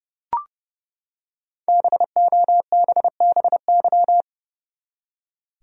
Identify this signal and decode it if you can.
Morse code, heard as a keyed tone, about 30 wpm, 700 Hz. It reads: BOBBY
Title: USA Common Names - Max 5 Letters - 30wpm